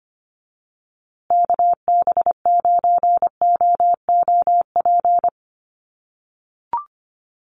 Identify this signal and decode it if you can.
Morse code, heard as a keyed tone, about 25 wpm, 700 Hz. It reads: KB9OOP
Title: USA Call Signs Repeated 3x 25wpm